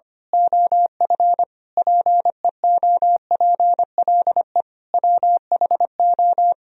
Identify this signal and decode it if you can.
Morse code, heard as a keyed tone, about 24 wpm, 700 Hz. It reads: OF PEOPLE WHO
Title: Speed-Racing - Sets of 5 Words 25wpm